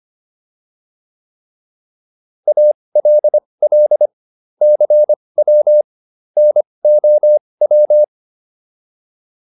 Morse code, heard as a keyed tone, about 25 wpm, spaced as for 20 wpm, 600 Hz. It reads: ALL CW NOW